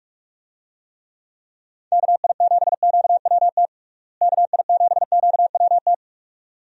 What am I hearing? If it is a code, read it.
Morse code, heard as a keyed tone, about 46 wpm, 700 Hz. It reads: KI7QWT KI7QWT
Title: USA Call Signs - 6 Characters - Repeated 2x - 45wpm